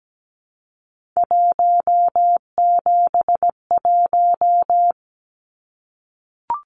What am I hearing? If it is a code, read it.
Morse code, heard as a keyed tone, about 17 wpm, 700 Hz. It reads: 171
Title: Numbers - 3-Digit 17wpm